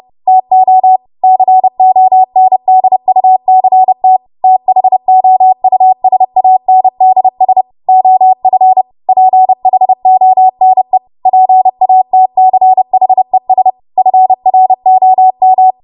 Morse code, heard as a keyed tone, about 30 wpm, 750 Hz. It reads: TO CONDUCT THOUSANDS OF PHONE PATCHES FROM
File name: Archive / 30 WPM / 30WPM